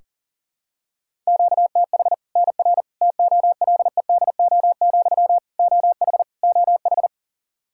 Morse code, heard as a keyed tone, about 40 wpm, 700 Hz. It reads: QTH NR TOLEDO, OH OH
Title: Practice QSO - Exchange 1 - Beginning - 40wpm